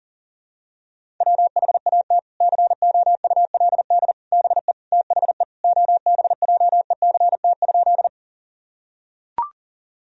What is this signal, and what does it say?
Morse code, heard as a keyed tone, about 40 wpm, 700 Hz. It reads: WHAT COULD BE THE OBJECT?
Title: Sentences from Top 1000 Words - Review Part 3 - 40wpm